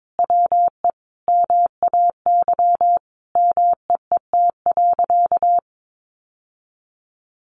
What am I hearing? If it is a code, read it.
Morse code, heard as a keyed tone, about 22 wpm, 700 Hz. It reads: WE MAY MEET.
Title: Sentences from Top 1100 Words 22wpm